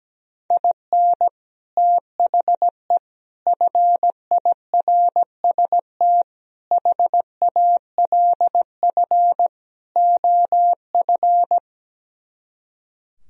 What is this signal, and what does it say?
Morse code, heard as a keyed tone, about 17 wpm, 700 Hz. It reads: IN THE FIRST HALF OF
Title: Speed-Racing - Sets of 5 Words 17wpm